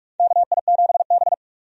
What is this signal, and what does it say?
Morse code, heard as a keyed tone, about 46 wpm, 700 Hz. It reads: KI7B
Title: USA Call Signs - 4 Characters - 45wpm